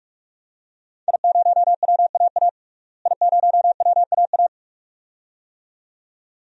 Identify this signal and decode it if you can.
Morse code, heard as a keyed tone, about 42 wpm, 700 Hz. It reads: I0WAA I0WAA
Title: DXCC-100 Call Signs - 5 Characters - Repeated 2x - 45wpm